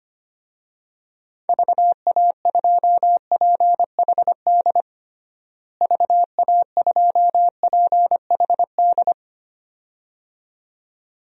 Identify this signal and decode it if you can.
Morse code, heard as a keyed tone, about 25 wpm, 700 Hz. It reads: VA2PHD VA2PHD
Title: DXCC-100 Call Signs - 6 Characters - Repeated 2x - 25wpm